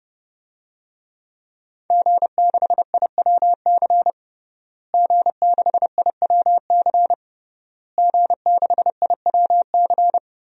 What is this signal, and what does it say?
Morse code, heard as a keyed tone, about 30 wpm, 700 Hz. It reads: G6IWC G6IWC G6IWC
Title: DXCC-100 Call Signs - 5 Characters - Repeated 3x - 30wpm